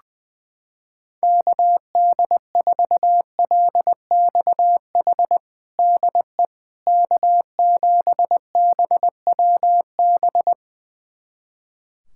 Morse code, heard as a keyed tone, about 21 wpm, 700 Hz. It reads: KD4LXH DE K7BWB
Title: USA Callsign DE Callsign 20wpm